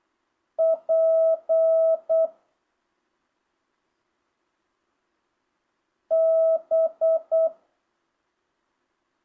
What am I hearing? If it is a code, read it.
Morse code, heard as a keyed tone, about 8 wpm, 650 Hz. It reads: P B
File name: jh8cbh / 2025cw